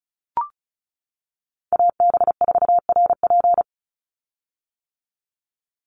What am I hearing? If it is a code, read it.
Morse code, heard as a keyed tone, about 36 wpm, 700 Hz. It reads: AB4RP
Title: USA Call Signs 35wpm